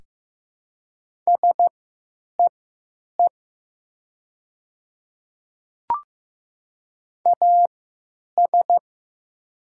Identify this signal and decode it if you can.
Morse code, heard as a keyed tone, about 15 wpm, spaced as for 8 wpm, 700 Hz. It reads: SEE AS
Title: 3x Character Spacing - Top 200 Words - 15wpm